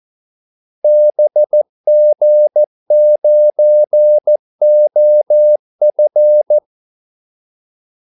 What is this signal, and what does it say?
Morse code, heard as a keyed tone, about 14 wpm, 600 Hz. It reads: BG9OF
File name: Callsigns-14wpm